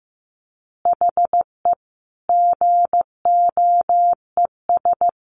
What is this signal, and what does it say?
Morse code, heard as a keyed tone, about 15 wpm, 700 Hz. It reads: HE GOES
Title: Additional Easy Phrases 15wpm